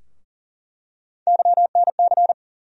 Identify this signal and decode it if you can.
Morse code, heard as a keyed tone, about 40 wpm, 700 Hz. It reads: YNC
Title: USA Call Sign Postfixes - 3 Letters 40wpm